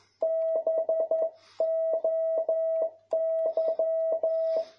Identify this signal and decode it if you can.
Morse code, heard as a keyed tone, about 11 wpm, 650 Hz. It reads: BOY
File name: jh8cbh / 2021cw